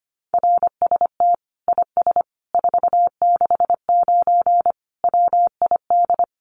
Morse code, heard as a keyed tone, about 25 wpm, 700 Hz. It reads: RST IS 469 WID
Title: Common QSO Phrases 25wpm